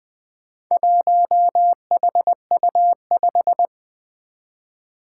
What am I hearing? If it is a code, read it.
Morse code, heard as a keyed tone, about 20 wpm, 700 Hz. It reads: 1HU5
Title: Four Challenging Letter-Number Combinations 20wpm